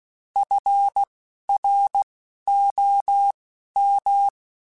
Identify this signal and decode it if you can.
Morse code, heard as a keyed tone, about 16 wpm, spaced as for 11 wpm, 800 Hz. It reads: FROM
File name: Quotes Spaced / Quotes Spaced-mp3s / HoofInMouth sp @15